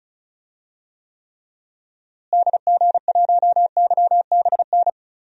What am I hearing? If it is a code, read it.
Morse code, heard as a keyed tone, about 36 wpm, 700 Hz. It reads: DG1YBN
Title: DXCC-100 Call Signs - 6 Characters - Repeated 2x - 35wpm